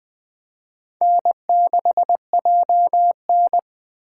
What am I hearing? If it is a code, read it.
Morse code, heard as a keyed tone, about 20 wpm, 700 Hz. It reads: N6JN
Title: Speed Racing - USA Call Signs 17wpm